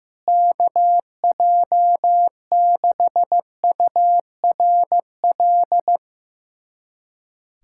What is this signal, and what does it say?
Morse code, heard as a keyed tone, about 15 wpm, 700 Hz. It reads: KJ6URL
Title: USA Callsign DE Callsign - 6 Characters - 15wpm